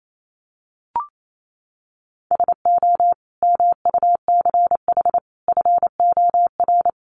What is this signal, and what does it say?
Morse code, heard as a keyed tone, about 28 wpm, 700 Hz. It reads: SO MUCH FOR
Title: Sets of 5 Words 28wpm